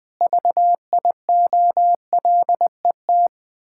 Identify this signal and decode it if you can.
Morse code, heard as a keyed tone, about 21 wpm, 700 Hz. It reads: VIOLET
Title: Pangram Challenge v2 - 20wpm